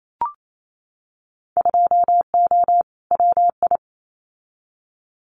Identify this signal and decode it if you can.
Morse code, heard as a keyed tone, about 28 wpm, 700 Hz. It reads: 2O WI
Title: Winter Field Day Exchanges 28wpm